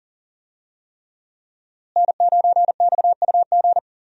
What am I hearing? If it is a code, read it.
Morse code, heard as a keyed tone, about 40 wpm, 700 Hz. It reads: N9XUG
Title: USA Call Signs - 5 Characters - Repeated 3x - 40wpm